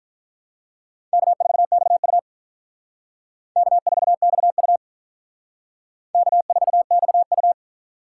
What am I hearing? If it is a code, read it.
Morse code, heard as a keyed tone, about 45 wpm, 700 Hz. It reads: K4XU K4XU K4XU